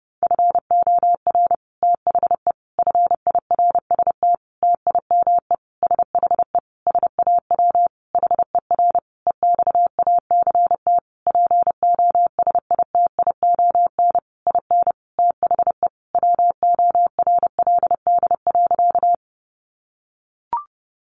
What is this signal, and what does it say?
Morse code, heard as a keyed tone, about 28 wpm, 700 Hz. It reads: FOR THE FIRST TIME SHE SAW HER EXACT POSITION IN THE WORLD.
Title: Sentences from Top 1300 Words 30wpm